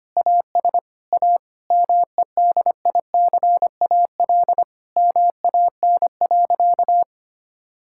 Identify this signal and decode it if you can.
Morse code, heard as a keyed tone, about 25 wpm, 700 Hz. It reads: AS A MEDICAL MAN.
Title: Sentences from Top 1000 Words - Review Part 5 - 25wpm